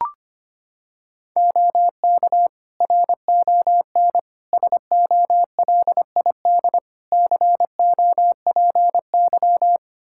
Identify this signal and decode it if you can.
Morse code, heard as a keyed tone, about 25 wpm, 700 Hz. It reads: OK RON SOLID COPY
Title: Practice QSO - Exchange 3 - More Info - 25wpm